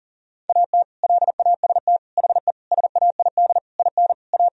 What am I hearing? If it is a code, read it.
Morse code, heard as a keyed tone, about 40 wpm, 700 Hz. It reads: AT LAST HE SAID IN A
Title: Sentences from Top 1000 Words - Review Part 1 - 40wpm